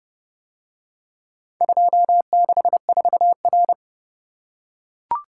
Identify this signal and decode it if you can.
Morse code, heard as a keyed tone, about 27 wpm, 700 Hz. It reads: 264R